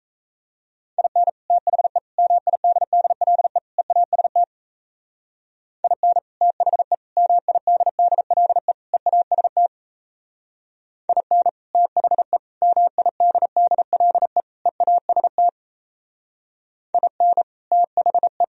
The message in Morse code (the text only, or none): IN THE MIDDLE EAST IN THE MIDDLE EAST IN THE MIDDLE EAST IN THE